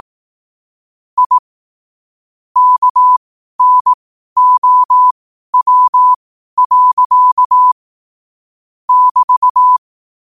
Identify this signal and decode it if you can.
Morse code, heard as a keyed tone, about 18 wpm, spaced as for 12 wpm, 1000 Hz. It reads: I KNOW. <BT>
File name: qotdfiles / 12wpm